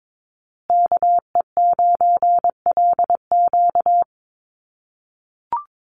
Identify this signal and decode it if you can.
Morse code, heard as a keyed tone, about 22 wpm, 700 Hz. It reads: KE9LQ
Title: USA Call Signs Repeated 3x 22wpm